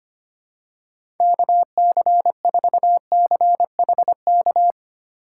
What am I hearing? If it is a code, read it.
Morse code, heard as a keyed tone, about 25 wpm, 700 Hz. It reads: KC4CHK
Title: USA Call Signs - 6 Characters - 3x Character Spacing - 25wpm